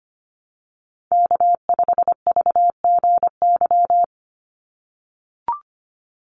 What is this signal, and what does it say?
Morse code, heard as a keyed tone, about 25 wpm, 700 Hz. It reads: K5VGY
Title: USA Call Signs 25wpm